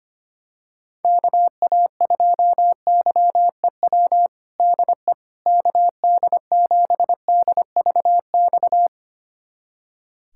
Morse code, heard as a keyed tone, about 25 wpm, 700 Hz. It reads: KA2YEW DE KD7DVX
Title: USA Callsign DE Callsign - 6 Characters - 25wpm